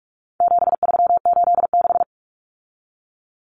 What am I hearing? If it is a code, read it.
Morse code, heard as a keyed tone, about 44 wpm, 700 Hz. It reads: 7386
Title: Common Cut Numbers - 4-Digit 45wpm